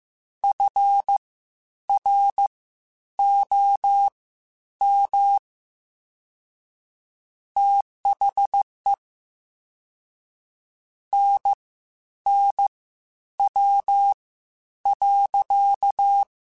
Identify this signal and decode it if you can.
Morse code, heard as a keyed tone, about 15 wpm, spaced as for 9 wpm, 800 Hz. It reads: FROM THE NNW.
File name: Spaced Characters / Spaced Characters-mp3s / Weather5 22 sp @13